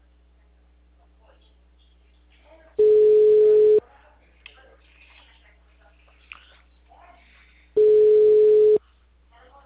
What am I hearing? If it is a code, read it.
Ringback tone (Europe, 425 Hz), with speech.